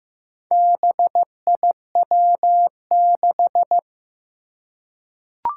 Morse code, heard as a keyed tone, about 15 wpm, 700 Hz. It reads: BIW6
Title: Four Challenging Letter-Number Combinations 15wpm